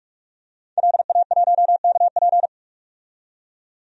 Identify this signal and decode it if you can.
Morse code, heard as a keyed tone, about 46 wpm, 700 Hz. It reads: LA1KP
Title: DXCC-100 Call Signs - 5 Characters - 45wpm